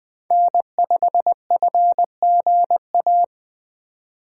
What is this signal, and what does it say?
Morse code, heard as a keyed tone, about 20 wpm, 700 Hz. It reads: N5FGA
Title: USA Call Signs - 2x Character Spacing - 20wpm